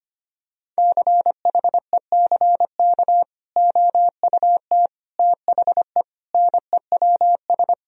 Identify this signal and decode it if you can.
Morse code, heard as a keyed tone, about 25 wpm, 700 Hz. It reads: CHECK OUT THE NEWS
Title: Sentences from Top 1000 Words - Review Part 4 - 25wpm